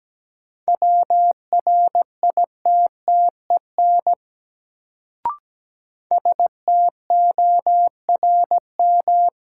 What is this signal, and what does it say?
Morse code, heard as a keyed tone, about 17 wpm, 700 Hz. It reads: WRITTEN STORM
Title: Sentences from Top 1500 Words - New Word Drill - 17wpm